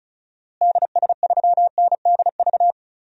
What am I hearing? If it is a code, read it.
Morse code, heard as a keyed tone, about 36 wpm, 700 Hz. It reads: DS3NDV
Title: DXCC-100 Call Signs - 6 Characters - Repeated 2x - 35wpm